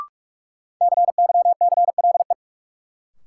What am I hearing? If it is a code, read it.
Morse code, heard as a keyed tone, about 42 wpm, 700 Hz. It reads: CYCLE
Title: Sentences from Top 900 Words - New Word Drill - 45wpm